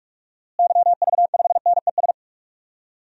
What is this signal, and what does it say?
Morse code, heard as a keyed tone, about 41 wpm, 700 Hz. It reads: YV5NES
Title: DXCC-100 Call Signs - 6 Characters - Repeated 2x - 45wpm